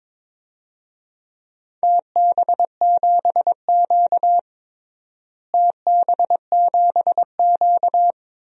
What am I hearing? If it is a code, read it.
Morse code, heard as a keyed tone, about 22 wpm, 700 Hz. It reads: TB7Q TB7Q
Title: DXCC-100 Call Signs - 4 Characters - Repeated 2x - 22wpm